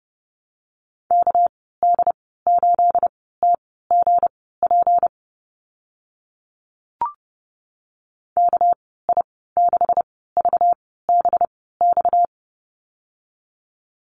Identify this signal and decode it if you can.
Morse code, heard as a keyed tone, about 30 wpm, spaced as for 17 wpm, 700 Hz. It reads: KD8TGP KI6VBX
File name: Lesson 37g - Letter x - Call Signs 30wpm